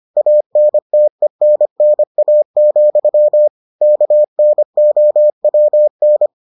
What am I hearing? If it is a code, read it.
Morse code, heard as a keyed tone, about 25 wpm, 600 Hz. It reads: ANTENNA, KNOWN